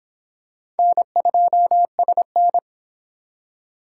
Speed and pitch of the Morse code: 25 wpm, 700 Hz

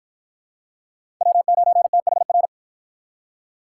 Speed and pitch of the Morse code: 53 wpm, 700 Hz

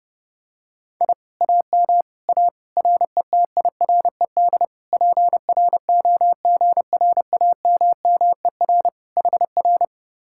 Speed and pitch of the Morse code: 32 wpm, 700 Hz

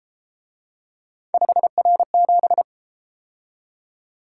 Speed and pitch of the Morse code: 31 wpm, 700 Hz

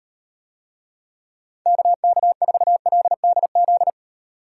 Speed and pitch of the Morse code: 38 wpm, 700 Hz

